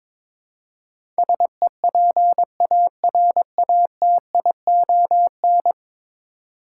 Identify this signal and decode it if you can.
Morse code, heard as a keyed tone, about 22 wpm, 700 Hz. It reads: SEPARATION
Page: morse-code-ninja-practice-22wpm